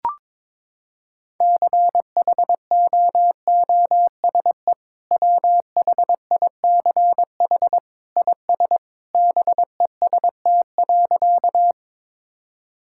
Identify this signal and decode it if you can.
Morse code, heard as a keyed tone, about 22 wpm, 700 Hz. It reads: CHOOSE WHICH IS BEST.
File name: practice / 22wpm / Sentences with 2-Word Phrases - 22wpm